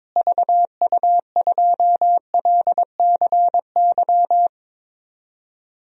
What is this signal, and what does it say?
Morse code, heard as a keyed tone, about 22 wpm, 700 Hz. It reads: VU2LCY
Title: DXCC-100 Call Signs - 6 Characters - Repeated 2x - 22wpm